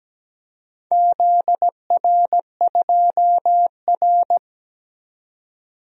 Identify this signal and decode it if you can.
Morse code, heard as a keyed tone, about 17 wpm, 700 Hz. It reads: ZR2R
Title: DXCC-100 Call Signs - 4 Characters - 17wpm